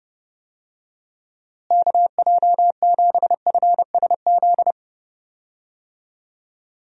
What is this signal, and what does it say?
Morse code, heard as a keyed tone, about 30 wpm, 700 Hz. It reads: KJ7FSZ
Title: USA Call Signs - 6 Characters - Repeated 2x - 30wpm